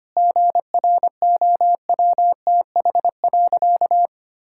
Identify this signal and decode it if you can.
Morse code, heard as a keyed tone, about 25 wpm, 700 Hz. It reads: GROWTH.
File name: practice / 25wpm / Difficult Sentences with Hints - 25wpm